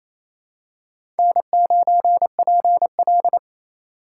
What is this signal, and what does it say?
Morse code, heard as a keyed tone, about 28 wpm, 700 Hz. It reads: N9PL
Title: USA Call Signs - 4 Characters - 4x Character Spacing - 28wpm